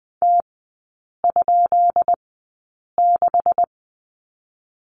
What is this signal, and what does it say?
Morse code, heard as a keyed tone, about 20 wpm, spaced as for 8 wpm, 690 Hz. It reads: T?6
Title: 133 Lesson 22e - Question Mark - Three Characters 20wpm